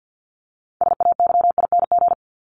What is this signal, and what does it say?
Morse code, heard as a keyed tone, about 56 wpm, 700 Hz. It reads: SAYING
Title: ING Suffix - Top 100 Words with Base Words 50wpm